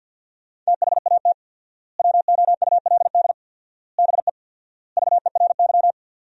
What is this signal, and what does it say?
Morse code, heard as a keyed tone, about 50 wpm, 700 Hz. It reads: THAT WOULD BE VERY
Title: 4x Word-Spacing - Sentences from Top 200 Words - 50wpm